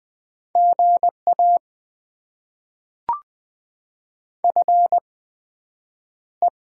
Morse code, heard as a keyed tone, about 20 wpm, spaced as for 9 wpm, 700 Hz. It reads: GA FE
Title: Top 20 QSO Elements - 8x Character Spacing - 20wpm